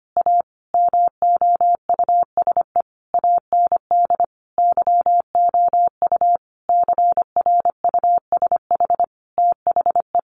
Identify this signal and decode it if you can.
Morse code, heard as a keyed tone, about 25 wpm, 700 Hz. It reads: A MOUSE AND YOU CRUSH THE
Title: A Sound Of Thunder 25wpm